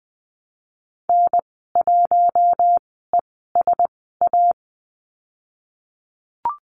Morse code, heard as a keyed tone, about 20 wpm, spaced as for 15 wpm, 700 Hz. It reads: N1ESA
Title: 055 Lesson 10g - Letter r - Call Signs 20wpm